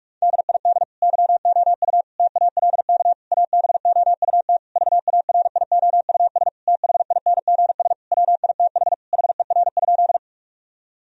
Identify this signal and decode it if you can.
Morse code, heard as a keyed tone, about 46 wpm, 700 Hz. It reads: DID YOU TALK ABOUT VARIOUS THINGS WITH HER?
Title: Sentences from Top 1000 Words - Review Part 4 - 45wpm